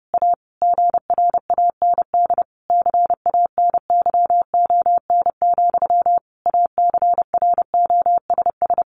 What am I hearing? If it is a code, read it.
Morse code, heard as a keyed tone, about 30 wpm, 700 Hz. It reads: A GRAND CANYON, ACROSS
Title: A Sound Of Thunder 30wpm